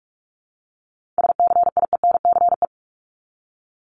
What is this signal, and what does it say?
Morse code, heard as a keyed tone, about 46 wpm, 700 Hz. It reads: SCIENCE
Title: Multi-letter Phonemes Repeated 3x 45wpm